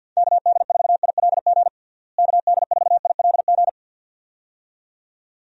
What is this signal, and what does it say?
Morse code, heard as a keyed tone, about 51 wpm, 700 Hz. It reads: KD4ILG KD4ILG
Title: USA Call Signs - 6 Characters - Repeated 2x - 50wpm